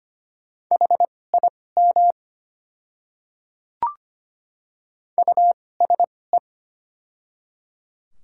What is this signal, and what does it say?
Morse code, heard as a keyed tone, about 25 wpm, spaced as for 18 wpm, 700 Hz. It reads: HIM USE
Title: 2x Character Spacing - Top 100 Words - 25wpm